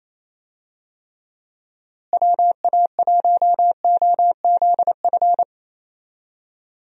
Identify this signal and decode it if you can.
Morse code, heard as a keyed tone, about 27 wpm, 700 Hz. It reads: WA1OZF
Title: USA Call Signs - 6 Characters - Repeated 3x - 28wpm